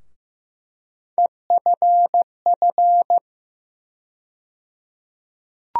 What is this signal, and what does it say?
Morse code, heard as a keyed tone, about 15 wpm, 700 Hz. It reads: EFF
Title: USA Call Sign Postfixes - 3 Letters 15wpm